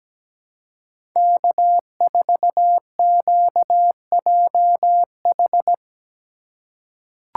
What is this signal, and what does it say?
Morse code, heard as a keyed tone, about 17 wpm, 700 Hz. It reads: K4QJH
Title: USA Call Signs - 5 Characters - 3x Character Spacing - 17wpm